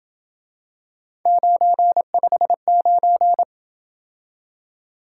Morse code, heard as a keyed tone, about 28 wpm, 700 Hz. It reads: 959